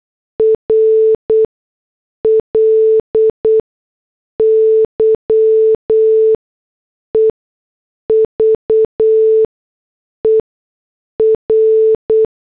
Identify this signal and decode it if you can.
Morse code, heard as a keyed tone, about 8 wpm, spaced as for 6 wpm, 440 Hz. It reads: RLY EVER